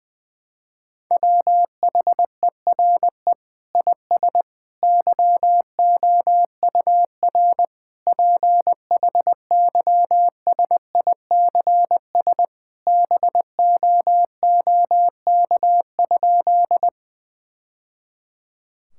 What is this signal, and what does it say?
Morse code, heard as a keyed tone, about 21 wpm, 700 Hz. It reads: WHERE IS YOUR PHYSICS BOOK?
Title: Sentences from Top 1000 Words - Review Part 5 - 20wpm